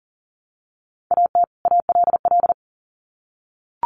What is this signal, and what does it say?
Morse code, heard as a keyed tone, about 40 wpm, 700 Hz. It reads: AT ALL
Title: Sets of 2 Words 40wpm